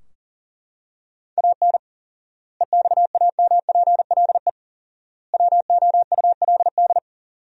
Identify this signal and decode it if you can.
Morse code, heard as a keyed tone, about 40 wpm, 700 Hz. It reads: AN EXAMPLE WOULD
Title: 4x Word-Spacing - Sentences from Top 400 Words - 40wpm